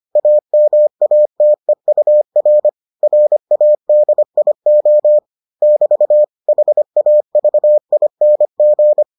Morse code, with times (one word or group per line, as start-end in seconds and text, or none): 0.2-2.7 AMATEUR
3.0-5.2 RADIO
5.6-6.2 <BT>
6.5-9.0 HAVING